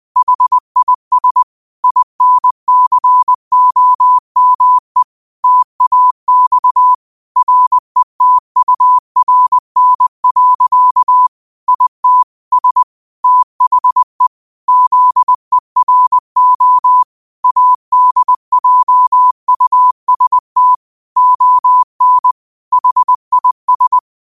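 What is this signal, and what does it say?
Morse code, heard as a keyed tone, about 20 wpm, 1000 Hz. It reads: HIS INCOME TAX RETURN. IT S THE ZERO ADJUST ON HIS
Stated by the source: 20 WPM Morse code quotes, QOTD at 20 WPM.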